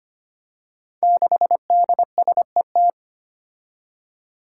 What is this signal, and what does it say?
Morse code, heard as a keyed tone, about 25 wpm, 700 Hz. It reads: 6DSET
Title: Five Challenging Letter-Number Combinations 25wpm